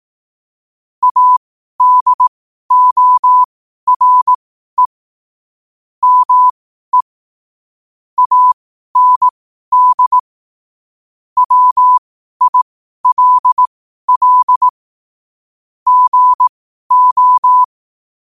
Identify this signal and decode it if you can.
Morse code, heard as a keyed tone, about 18 wpm, spaced as for 12 wpm, 1000 Hz. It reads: ADORE ME AND WILL GO
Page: Quotes for Fri, 15 Aug 2025 in Morse Code at 12 words per minute.